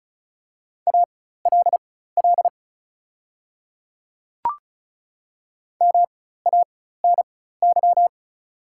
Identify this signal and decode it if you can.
Morse code, heard as a keyed tone, about 36 wpm, spaced as for 16 wpm, 690 Hz. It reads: ALL MANY
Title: 4x Character Spacing - Top 100 Words - 35wpm